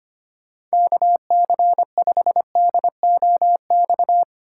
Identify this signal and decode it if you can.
Morse code, heard as a keyed tone, about 25 wpm, 700 Hz. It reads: KC5DOX
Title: USA Call Signs - 6 Characters - 2x Character Spacing - 25wpm